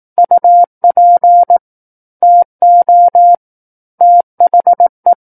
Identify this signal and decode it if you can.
Morse code, heard as a keyed tone, about 18 wpm, 700 Hz. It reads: UP TO THE
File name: 18 WPM